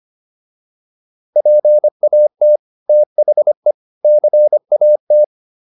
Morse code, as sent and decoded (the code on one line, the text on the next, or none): .--. .- - / - .... / -.-. .- -
PAT THE CAT